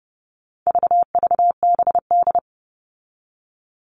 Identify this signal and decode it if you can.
Morse code, heard as a keyed tone, about 30 wpm, 700 Hz. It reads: VVBD
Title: All Cut Numbers - 4-Digit 30wpm